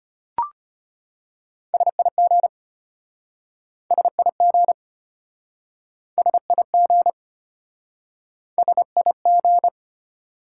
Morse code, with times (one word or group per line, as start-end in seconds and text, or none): 1.7-2.5 SIG
3.9-4.7 SIG
6.2-7.1 SIG
8.6-9.7 SIG